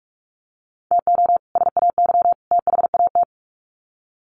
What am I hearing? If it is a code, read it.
Morse code, heard as a keyed tone, about 44 wpm, 700 Hz. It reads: TO SAY THAT
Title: Sets of 3 Words 45wpm